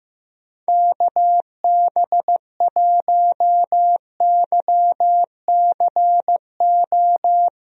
Morse code, heard as a keyed tone, about 16 wpm, 700 Hz. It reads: KB1YCO
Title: USA Call Signs - 6 Characters - Repeated 2x - 15wpm